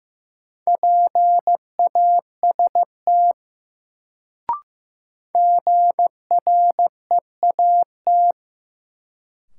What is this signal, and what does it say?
Morse code, heard as a keyed tone, about 15 wpm, 700 Hz. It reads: PAST GREAT
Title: Sets of 2 Words - New Word Drill - 15wpm